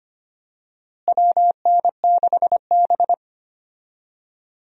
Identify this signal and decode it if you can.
Morse code, heard as a keyed tone, about 25 wpm, 700 Hz. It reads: WN6B
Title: USA Call Signs - 4 Characters - Repeated 2x - 25wpm